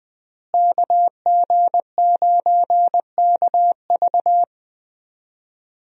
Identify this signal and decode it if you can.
Morse code, heard as a keyed tone, about 20 wpm, 700 Hz. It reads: KG9KV
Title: USA Call Signs - 5 Characters - Repeated 2x - 20wpm